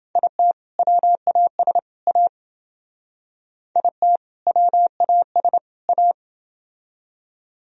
Morse code, heard as a keyed tone, about 29 wpm, 700 Hz. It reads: IT WAS A IT WAS A